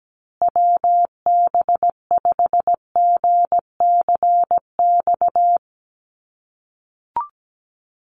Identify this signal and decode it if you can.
Morse code, heard as a keyed tone, about 17 wpm, 700 Hz. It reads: WB5GCX
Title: USA Call Signs Repeated 2x 17wpm